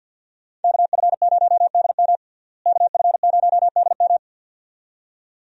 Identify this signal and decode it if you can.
Morse code, heard as a keyed tone, about 47 wpm, 700 Hz. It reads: KF0DM KF0DM
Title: USA Call Signs - 5 Characters - Repeated 2x - 50wpm